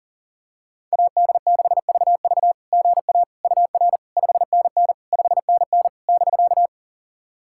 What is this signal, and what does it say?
Morse code, heard as a keyed tone, about 42 wpm, 700 Hz. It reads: AD6VV GA UR 5NN 5NN <BK>
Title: SOTA QSO Practice 40wpm